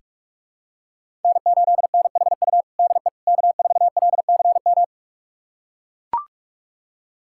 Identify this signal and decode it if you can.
Morse code, heard as a keyed tone, about 41 wpm, 700 Hz. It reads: N8NHU DE K4LCM
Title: USA Callsign DE Callsign - 5 Characters - 45wpm